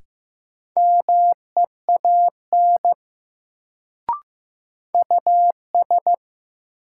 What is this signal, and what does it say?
Morse code, heard as a keyed tone, about 15 wpm, 700 Hz. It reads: MEAN US
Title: Sets of 2 Words - New Word Drill - 15wpm